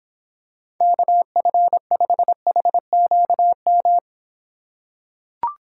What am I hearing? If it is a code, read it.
Morse code, heard as a keyed tone, about 26 wpm, 700 Hz. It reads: KF5HQM